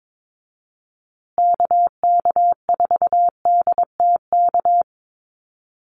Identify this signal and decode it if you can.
Morse code, heard as a keyed tone, about 22 wpm, 700 Hz. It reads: KK4DTK
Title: USA Call Signs Repeated 2x 22wpm